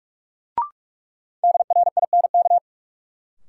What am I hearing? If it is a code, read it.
Morse code, heard as a keyed tone, about 42 wpm, 700 Hz. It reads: DRINK
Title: Sentences from Top 1500 Words - New Word Drill - 45wpm